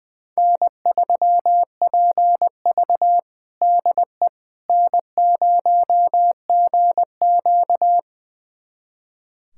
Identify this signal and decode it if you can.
Morse code, heard as a keyed tone, about 20 wpm, 700 Hz. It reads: N3PV DE N0GQ
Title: USA Callsign DE Callsign - 4 Characters - 20wpm